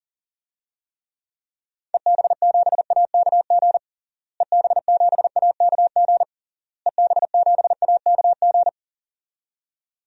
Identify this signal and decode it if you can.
Morse code, heard as a keyed tone, about 37 wpm, 700 Hz. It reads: EB7AKG EB7AKG EB7AKG
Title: DXCC-100 Call Signs - 6 Characters - Repeated 3x - 40wpm